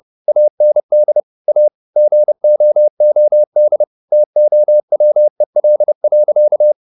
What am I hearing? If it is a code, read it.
Morse code, heard as a keyed tone, about 30 wpm, 600 Hz. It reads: AND A GOOD TOWEL.
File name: the old man and the sea / the old man and the sea 1 30wpm